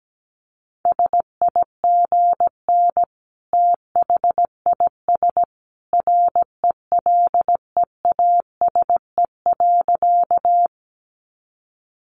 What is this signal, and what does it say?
Morse code, heard as a keyed tone, about 17 wpm, 700 Hz. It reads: SIGN THIS RELEASE.
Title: A Sound Of Thunder 17wpm